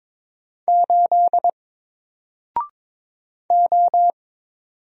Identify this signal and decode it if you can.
Morse code, heard as a keyed tone, about 22 wpm, 700 Hz. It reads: 8 O
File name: Single-Letter-Number-22wpm